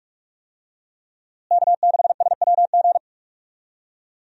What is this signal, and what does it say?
Morse code, heard as a keyed tone, about 46 wpm, 700 Hz. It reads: K6SWG
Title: USA Call Signs - 5 Characters - Repeated 3x - 45wpm